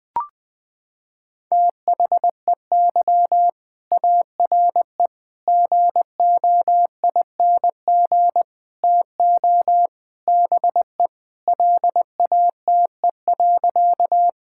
Morse code, heard as a keyed tone, about 21 wpm, 700 Hz. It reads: THEY ARE GOING TO BE LATE.
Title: Sentences with 4-Word Phrases - 20wpm